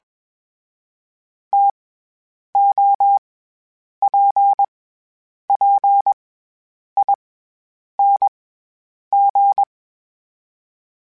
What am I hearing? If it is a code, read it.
Morse code, heard as a keyed tone, about 21 wpm, 800 Hz. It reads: T O P P I N G
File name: Variable Speed / Variable Speed-mp3s / TO sp @20